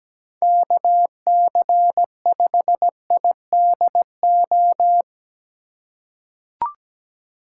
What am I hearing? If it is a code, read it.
Morse code, heard as a keyed tone, about 17 wpm, 700 Hz. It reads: KC5IDO
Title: USA Call Signs - 6 Characters - 3x Character Spacing - 17wpm